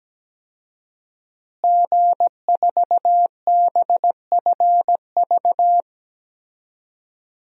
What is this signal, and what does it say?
Morse code, heard as a keyed tone, about 17 wpm, 700 Hz. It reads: G4BFV
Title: DXCC-100 Call Signs - 5 Characters - Repeated 2x - 17wpm